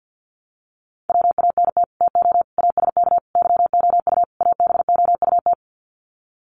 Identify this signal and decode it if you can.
Morse code, heard as a keyed tone, about 49 wpm, 700 Hz. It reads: WANT TO ASK YOU ABOUT
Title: Sets of 5 Words Encore2 50wpm